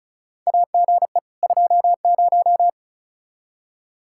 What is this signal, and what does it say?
Morse code, heard as a keyed tone, about 34 wpm, 700 Hz. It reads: AGE 20
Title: Practice QSO - Exchange 3 - More Info - 35wpm